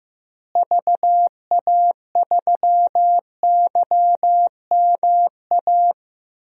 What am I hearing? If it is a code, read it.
Morse code, heard as a keyed tone, about 15 wpm, 700 Hz. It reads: VA3YMA
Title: DXCC-100 Call Signs - 6 Characters - Repeated 2x - 15wpm